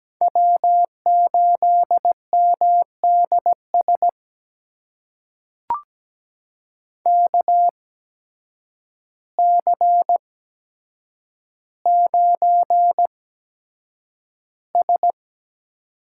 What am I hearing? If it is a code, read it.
Morse code, heard as a keyed tone, about 17 wpm, spaced as for 8 wpm, 700 Hz. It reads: W8MDS KC9S
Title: USA Call Signs - 8x Character Spacing - 17wpm